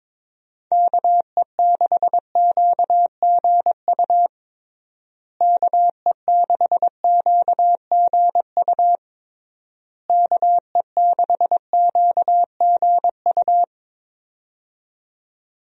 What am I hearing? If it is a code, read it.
Morse code, heard as a keyed tone, about 22 wpm, 700 Hz. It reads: KE6QGU KE6QGU KE6QGU
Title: USA Call Signs - 6 Characters - Repeated 3x - 22wpm